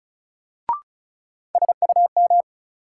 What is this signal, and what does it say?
Morse code, heard as a keyed tone, about 34 wpm, 700 Hz. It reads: SUM
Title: Sentences from Top 1000 Words - Review Part 6 - New Word Drill - 35wpm